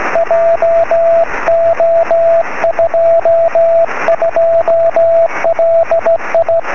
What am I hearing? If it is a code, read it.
Morse code, heard as a keyed tone, about 15 wpm, 670 Hz, with radio noise. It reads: JO22LI
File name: sm7lcb / 20 23cm / 1544 pa3bfk 23cm